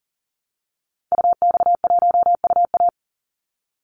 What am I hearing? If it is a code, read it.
Morse code, heard as a keyed tone, about 40 wpm, 700 Hz. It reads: UX1UA
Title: Call Signs - 40wpm